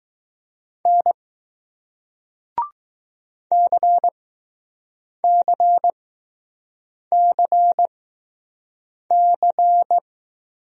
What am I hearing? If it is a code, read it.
Morse code, heard as a keyed tone, about 20 wpm, spaced as for 6 wpm, 700 Hz. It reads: N CCCC